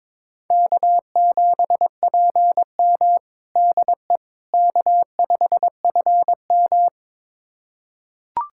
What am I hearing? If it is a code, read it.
Morse code, heard as a keyed tone, about 22 wpm, 700 Hz. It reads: K7PM DE K5FM
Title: USA Callsign DE Callsign - 4 Characters - 22wpm